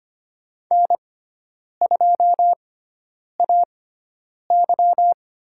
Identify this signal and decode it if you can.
Morse code, heard as a keyed tone, about 25 wpm, spaced as for 9 wpm, 700 Hz. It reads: N2AY
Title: USA Call Signs - 5 Characters - 6x Character Spacing - 25wpm